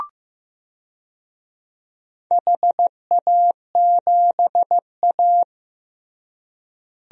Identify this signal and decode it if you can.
Morse code, heard as a keyed tone, about 15 wpm, 700 Hz. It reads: HA7A
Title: DXCC-100 Call Signs - 4 Characters - Repeated 3x - 15wpm